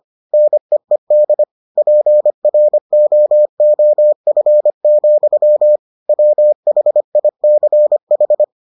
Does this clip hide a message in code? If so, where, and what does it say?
Yes: Morse code from 0.3 to 8.4 s, heard as a keyed tone, about 25 wpm, 600 Hz. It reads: NEED PROOF, WHICH